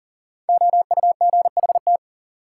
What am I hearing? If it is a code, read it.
Morse code, heard as a keyed tone, about 40 wpm, 700 Hz. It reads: OUGHT
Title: Sets of 5 Words Encore - New Word Drill - 40wpm